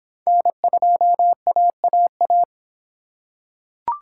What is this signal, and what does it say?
Morse code, heard as a keyed tone, about 25 wpm, 700 Hz. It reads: N2AAA